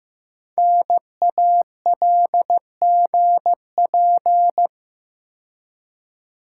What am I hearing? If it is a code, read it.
Morse code, heard as a keyed tone, about 15 wpm, 700 Hz. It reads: NALGP
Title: Code Groups - 5 Letters - 15wpm